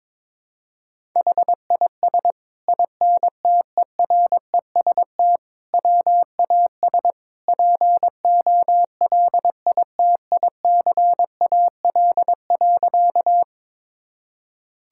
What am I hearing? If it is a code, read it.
Morse code, heard as a keyed tone, about 22 wpm, 700 Hz. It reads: HIS INTEREST WAS POLITICAL.
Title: Sentences from Top 1000 Words - Review Part 5 - 22wpm